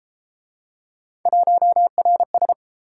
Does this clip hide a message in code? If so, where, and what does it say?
Yes: Morse code from 1.3 to 2.5 s, heard as a keyed tone, about 28 wpm, 700 Hz. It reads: 1RS